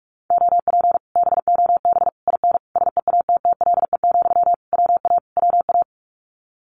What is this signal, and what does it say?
Morse code, heard as a keyed tone, about 46 wpm, 700 Hz. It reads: OP BOB IN SEATTLE, WA WA
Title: Common QSO Phrases Encore 45wpm